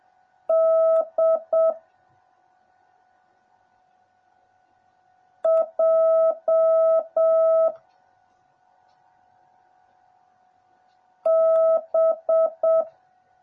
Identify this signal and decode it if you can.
Morse code, heard as a keyed tone, about 7 wpm, 650 Hz. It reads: D J B
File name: jh8cbh / 2021cw / cw09a